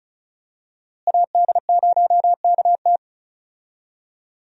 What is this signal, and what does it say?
Morse code, heard as a keyed tone, about 34 wpm, 700 Hz. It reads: AD0KT
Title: USA Call Signs - 5 Characters - Repeated 2x - 35wpm